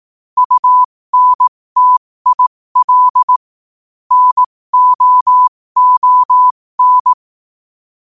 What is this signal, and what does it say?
Morse code, heard as a keyed tone, about 18 wpm, spaced as for 15 wpm, 1000 Hz. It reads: UNTIL NOON
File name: qotdfiles / 15wpm